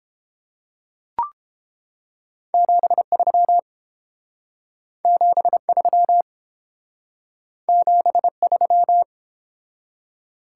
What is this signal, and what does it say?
Morse code, heard as a keyed tone, about 31 wpm, 700 Hz. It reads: 73 73 73